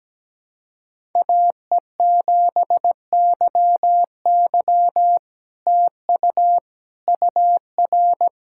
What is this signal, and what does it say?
Morse code, heard as a keyed tone, about 17 wpm, 700 Hz. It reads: AE7YY TU UR
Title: SOTA QSO Practice 17wpm